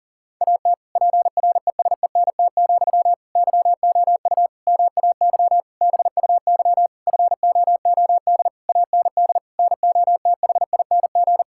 Morse code, heard as a keyed tone, about 40 wpm, 700 Hz. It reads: AT PRESENT, YOU MAY BUY FOOD AND NOTHING
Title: Sentences from Top 1000 Words - Review Part 5 - 40wpm